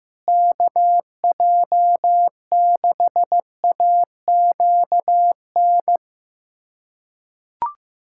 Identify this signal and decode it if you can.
Morse code, heard as a keyed tone, about 15 wpm, 700 Hz. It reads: KJ6AQN
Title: USA Callsign DE Callsign - 6 Characters - 15wpm